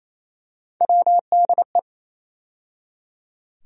Morse code, heard as a keyed tone, about 27 wpm, 700 Hz. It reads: WDE
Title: USA Call Sign Postfixes - 3 Letters 28wpm